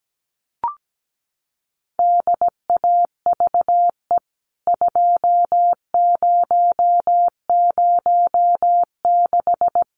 Morse code, heard as a keyed tone, about 17 wpm, 700 Hz. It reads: DAVE 2006
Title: CWT Exchange 17wpm